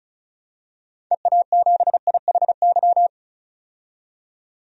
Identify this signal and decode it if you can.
Morse code, heard as a keyed tone, about 34 wpm, 700 Hz. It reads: EA7IHY
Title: DXCC-100 Call Signs - 6 Characters - 35wpm